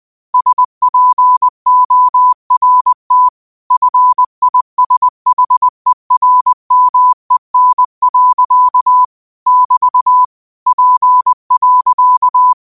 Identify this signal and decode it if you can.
Morse code, heard as a keyed tone, about 20 wpm, 1000 Hz. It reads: SPORT FISHERMEN. <BT> P.
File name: qotdfiles / 20wpm